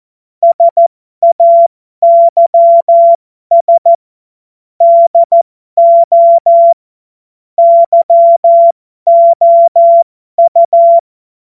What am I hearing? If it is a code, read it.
Morse code, heard as a keyed tone, about 14 wpm, spaced as for 12 wpm, 680 Hz. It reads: SAYS DO YOU
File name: Morsekurs / Witze